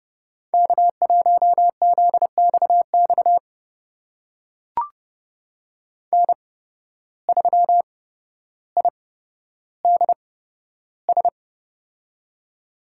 Morse code, heard as a keyed tone, about 30 wpm, spaced as for 13 wpm, 700 Hz. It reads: K1ZXX N3IDS
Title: USA Call Signs - 5 Characters - 8x Character Spacing - 30wpm